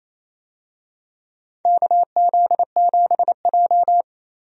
Morse code, heard as a keyed tone, about 27 wpm, 700 Hz. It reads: KZ7J
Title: USA Call Signs - 4 Characters - Repeated 3x - 28wpm